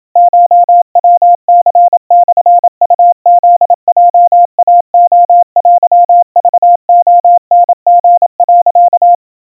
Morse code (Japanese Wabun code), heard as a keyed tone, about 27 wpm, 700 Hz. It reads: コヤニモウフヲイレテクレタリ、
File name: kawa / Morse sample / kamyu 100 cpm 700 Hz Crispy